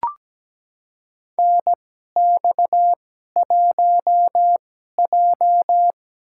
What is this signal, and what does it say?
Morse code, heard as a keyed tone, about 17 wpm, spaced as for 12 wpm, 700 Hz. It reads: NX1J
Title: USA Call Signs - 4 Characters - 2x Character Spacing - 17wpm